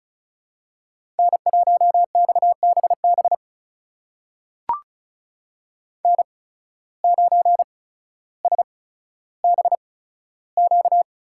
Morse code, heard as a keyed tone, about 34 wpm, spaced as for 15 wpm, 690 Hz. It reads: N1XBB N9SBQ
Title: USA Call Signs - 5 Characters - 8x Character Spacing - 35wpm